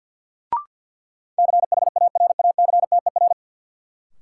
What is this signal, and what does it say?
Morse code, heard as a keyed tone, about 51 wpm, 700 Hz. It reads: CHARACTER
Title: Sentences from Top 1200 Words - New Word Drill Alt - 50wpm